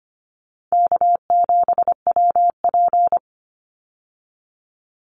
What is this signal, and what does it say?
Morse code, heard as a keyed tone, about 25 wpm, 700 Hz. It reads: K7WP
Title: Easy Call Signs Repeated 2x - 25wpm